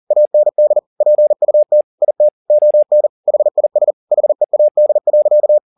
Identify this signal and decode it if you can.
Morse code, heard as a keyed tone, about 40 wpm, 600 Hz. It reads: AND PUT IT ON HIS HEAD.
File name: the old man and the sea 2 40wpm